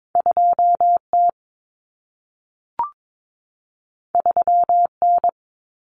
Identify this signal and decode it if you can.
Morse code, heard as a keyed tone, about 22 wpm, 700 Hz. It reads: 2T 3N
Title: Common Cut Numbers - 2-Digit 22wpm